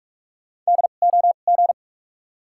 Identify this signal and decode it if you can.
Morse code, heard as a keyed tone, about 42 wpm, spaced as for 31 wpm, 700 Hz. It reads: DOG
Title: 2x Character Spacing - Top 500 Words - 45wpm